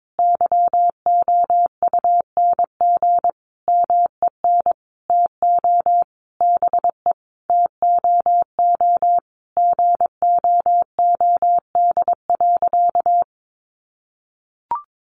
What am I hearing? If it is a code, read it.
Morse code, heard as a keyed tone, about 22 wpm, 700 Hz. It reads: YOUNG MEN TO BE TOO GOOD.
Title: Sentences from Top 1200 Words 22wpm